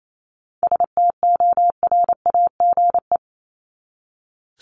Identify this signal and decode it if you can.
Morse code, heard as a keyed tone, about 28 wpm, 700 Hz. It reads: STORAGE
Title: Top 1251-1500 Words 28wpm